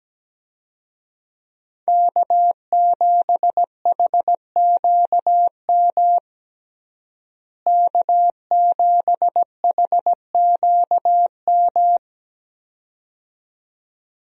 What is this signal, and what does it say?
Morse code, heard as a keyed tone, about 17 wpm, 700 Hz. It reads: K7HQM K7HQM
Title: USA Call Signs - 5 Characters - Repeated 2x - 17wpm